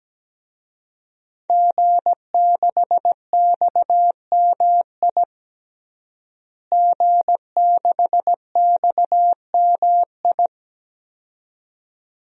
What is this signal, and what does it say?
Morse code, heard as a keyed tone, about 17 wpm, 700 Hz. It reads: G6XMI G6XMI
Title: DXCC-100 Call Signs - 5 Characters - Repeated 2x - 17wpm